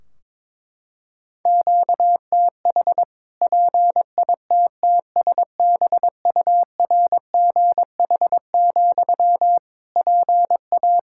Morse code, heard as a keyed tone, about 22 wpm, 700 Hz. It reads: QTH PITTSBURGH, PA
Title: Practice QSO - Exchange 1 - Beginning - 22wpm